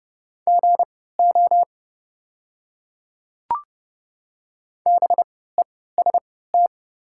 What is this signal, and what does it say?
Morse code, heard as a keyed tone, about 30 wpm, spaced as for 17 wpm, 690 Hz. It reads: GO BEST
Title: 3x Character Spacing - Top 500 Words - 30wpm